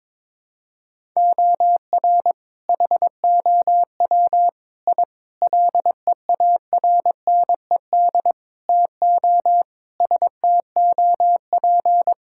Morse code, heard as a keyed tone, about 22 wpm, 700 Hz. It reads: OR HOW I LEARNED TO STOP
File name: practice / 22wpm / Sentences with 8-Word Phrases - 22wpm